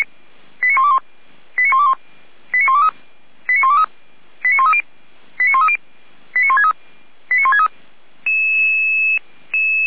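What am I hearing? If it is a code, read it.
Five-tone selective call (ZVEI), heard as quick runs of five tones, tones of about 70 ms. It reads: 89121 89121 89123 89123 89130 89130 89163 89163
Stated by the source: ZVEI five-tone call 89022 received via SDR, Feuerwehr Orlamünde fire brigade.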